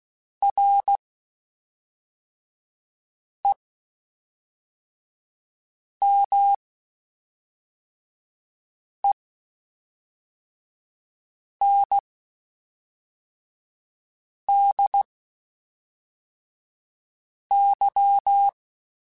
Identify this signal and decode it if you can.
Morse code, heard as a keyed tone, about 16 wpm, spaced as for 3 wpm, 800 Hz. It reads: REMENDY
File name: Quotes Spaced / Quotes Spaced-mp3s / Temper sp @5